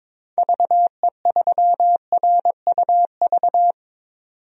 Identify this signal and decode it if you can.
Morse code, heard as a keyed tone, about 22 wpm, 700 Hz. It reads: VE3RUV
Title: DXCC-100 Call Signs - 6 Characters - 22wpm